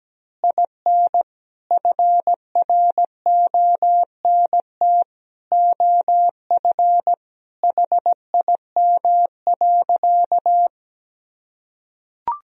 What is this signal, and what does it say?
Morse code, heard as a keyed tone, about 17 wpm, 700 Hz. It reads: IN FRONT OF HIM.
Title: Sentences from Top 1000 Words - Review Part 2 - 17wpm